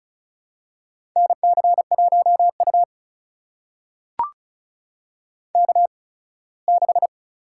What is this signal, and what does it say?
Morse code, heard as a keyed tone, about 36 wpm, spaced as for 21 wpm, 700 Hz. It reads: NC1U K6
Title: USA Call Signs - 8x Character Spacing - 35wpm